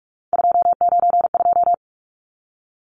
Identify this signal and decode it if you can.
Morse code, heard as a keyed tone, about 44 wpm, 700 Hz. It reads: 292
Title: Numbers - 3-Digit 45wpm